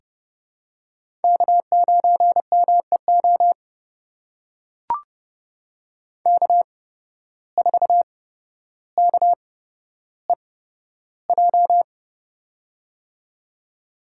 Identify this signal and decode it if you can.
Morse code, heard as a keyed tone, about 29 wpm, spaced as for 13 wpm, 700 Hz. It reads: K9MEO K4KEJ
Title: USA Call Signs - 5 Characters - 8x Character Spacing - 30wpm